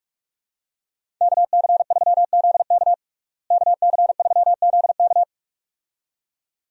Morse code, heard as a keyed tone, about 47 wpm, 700 Hz. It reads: KC3ZK KC3ZK
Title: USA Call Signs - 5 Characters - Repeated 2x - 45wpm